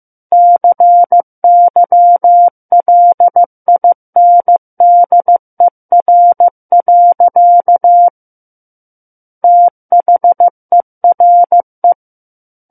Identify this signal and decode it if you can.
Morse code, heard as a keyed tone, about 15 wpm, 700 Hz. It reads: CYLINDER. THERE
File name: Chapter 6 - 15 WPM